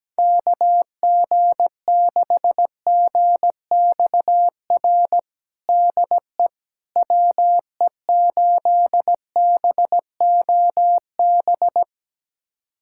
Morse code, heard as a keyed tone, about 17 wpm, 700 Hz. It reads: KG6GXR DE WE8BOB
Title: USA Callsign DE Callsign 17wpm